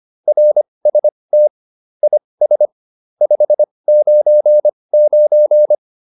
Morse code, heard as a keyed tone, about 25 wpm, spaced as for 20 wpm, 600 Hz. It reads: RST IS 599